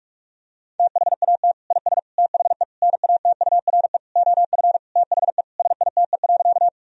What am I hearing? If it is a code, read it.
Morse code, heard as a keyed tone, about 41 wpm, 700 Hz. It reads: THAT IS THE NATURE OF THE SITE.
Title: Sentences from Top 1000 Words - Review Part 4 - 45wpm